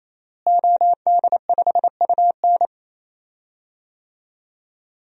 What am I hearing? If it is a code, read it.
Morse code, heard as a keyed tone, about 28 wpm, 700 Hz. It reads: OD5UN
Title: DXCC-100 Call Signs - 5 Characters - Repeated 3x - 28wpm